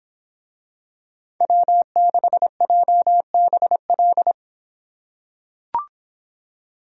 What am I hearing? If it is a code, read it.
Morse code, heard as a keyed tone, about 24 wpm, 700 Hz. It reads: W6JBL